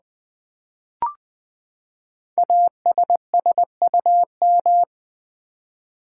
Morse code, heard as a keyed tone, about 20 wpm, 700 Hz. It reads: ASSUM
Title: Build A Word - 5 Letters At A Time - 10-Letter Words - 20wpm